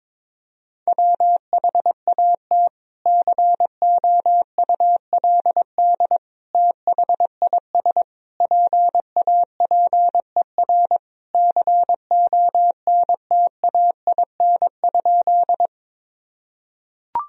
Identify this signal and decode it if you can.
Morse code, heard as a keyed tone, about 22 wpm, 700 Hz. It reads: WHAT COULD THIS PAPER CONTAIN?
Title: Sentences from Top 1000 Words - Review Part 2 - 22wpm